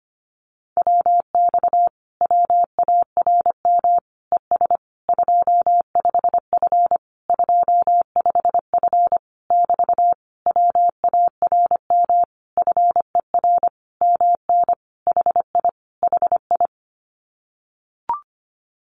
Morse code, heard as a keyed tone, about 25 wpm, 700 Hz. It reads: WX WARM ES 25F 25F <BT> WARM FER MN HI HI
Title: Common QSO Phrases Encore 25wpm